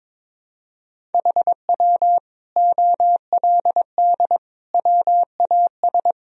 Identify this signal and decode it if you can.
Morse code, heard as a keyed tone, about 22 wpm, 700 Hz. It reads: HW OLD WAS
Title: Sentences from Top 200 Words and Abbreviations 22wpm